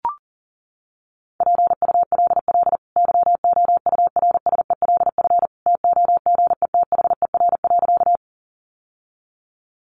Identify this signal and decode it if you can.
Morse code, heard as a keyed tone, about 40 wpm, 700 Hz. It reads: PULL YOURSELF TOGETHER.
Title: Sentences from Top 600 Words 40wpm